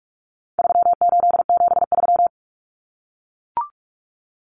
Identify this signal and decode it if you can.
Morse code, heard as a keyed tone, about 44 wpm, 700 Hz. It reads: 3873
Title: Numbers - 4-Digit 45wpm